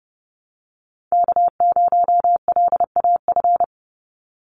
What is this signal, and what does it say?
Morse code, heard as a keyed tone, about 30 wpm, 700 Hz. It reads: K0LAF
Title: NA SOTA Chaser Callsigns 30wpm